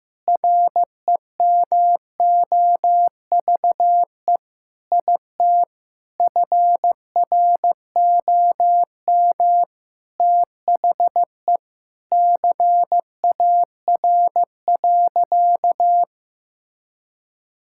Morse code, heard as a keyed tone, about 15 wpm, 700 Hz. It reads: REMOVE IT FROM THE CAR.
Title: Sentences from Top 1000 Words - Review Part 5 - 15wpm